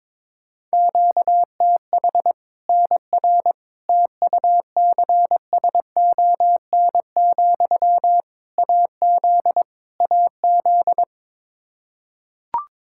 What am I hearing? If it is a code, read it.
Morse code, heard as a keyed tone, about 22 wpm, 700 Hz. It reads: QTH NR TUCSON, AZ AZ
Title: Practice QSO - Exchange 1 - Beginning - 22wpm